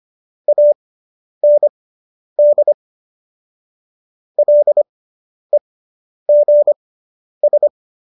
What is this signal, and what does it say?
Morse code, heard as a keyed tone, about 25 wpm, spaced as for 10 wpm, 590 Hz. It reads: AND LEGS